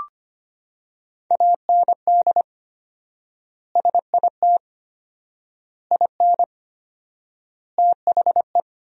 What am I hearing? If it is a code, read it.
Morse code, heard as a keyed tone, about 25 wpm, 700 Hz. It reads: AND SIT IN THE
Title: 4x Word-Spacing - Sentences from Top 500 Words - 25wpm